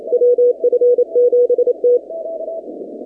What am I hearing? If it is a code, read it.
Morse code, heard as a keyed tone, about 27 wpm, 490 Hz. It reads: WF7T
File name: 2019-cw-wpx-kt5j-k5pi